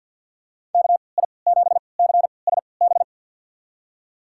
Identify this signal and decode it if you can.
Morse code, heard as a keyed tone, about 53 wpm, spaced as for 29 wpm, 700 Hz. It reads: KI7CSB
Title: USA Call Signs - 6 Characters - 3x Character Spacing - 50wpm